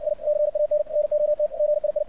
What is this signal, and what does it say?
Morse code, heard as a keyed tone, about 60 wpm, 600 Hz, with radio noise. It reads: I CAN NOT GII
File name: archived / CRSnet